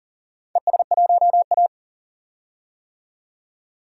Morse code, heard as a keyed tone, about 40 wpm, 700 Hz. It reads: ES1A
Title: DXCC-100 Call Signs - 4 Characters - 40wpm